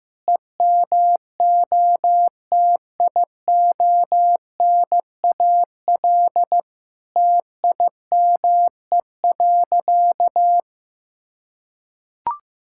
Morse code, heard as a keyed tone, about 15 wpm, 700 Hz. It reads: EMOTIONAL TIME.
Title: Sentences from Top 1000 Words - Review Part 5 - 15wpm